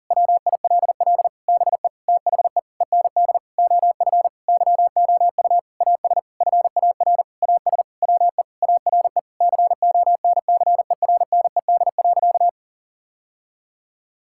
Morse code, heard as a keyed tone, about 40 wpm, 700 Hz. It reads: WILL BE THE END OF YOU AS FAR AS WE ARE CONCERNED.
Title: Sentences from Top 1000 Words - Review Part 6 - 40wpm